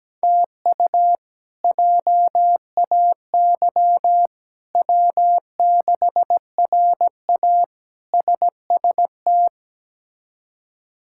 Morse code, heard as a keyed tone, about 17 wpm, 700 Hz. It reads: TU JAY W6RA SST
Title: K1USN SST QSO Practice - 17wpm